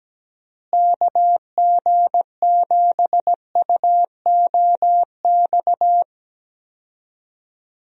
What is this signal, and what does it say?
Morse code, heard as a keyed tone, about 17 wpm, 700 Hz. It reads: KG7UOX
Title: USA Call Signs - 6 Characters - Repeated 2x - 17wpm